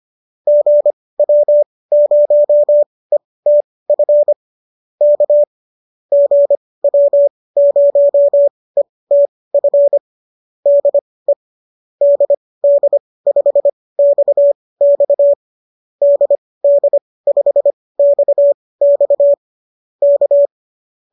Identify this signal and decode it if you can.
Morse code, heard as a keyed tone, about 25 wpm, spaced as for 18 wpm, 590 Hz. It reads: GW0ETF K GW0ETF DE DD5XX DD5XX K
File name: qso218 18